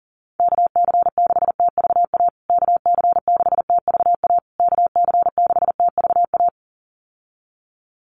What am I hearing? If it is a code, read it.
Morse code, heard as a keyed tone, about 40 wpm, 700 Hz. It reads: KC6TVA KC6TVA KC6TVA
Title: USA Call Signs Repeated 3x 40wpm